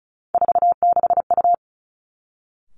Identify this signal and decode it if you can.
Morse code, heard as a keyed tone, about 36 wpm, 700 Hz. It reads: 46U
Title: Three Challenging Letter-Number Combinations 35wpm